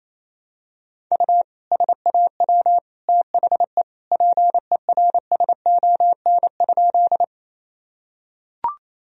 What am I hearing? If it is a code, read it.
Morse code, heard as a keyed tone, about 28 wpm, 700 Hz. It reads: U SAW THE PERSON?
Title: Sentences from Top 400 Words and Abbreviations 28wpm